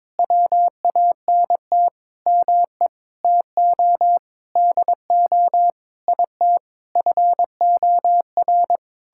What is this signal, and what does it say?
Morse code, heard as a keyed tone, about 21 wpm, 700 Hz. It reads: WANT ME TO DO IT FOR
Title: Sentences from Top 1000 Words - Review Part 6 - 22wpm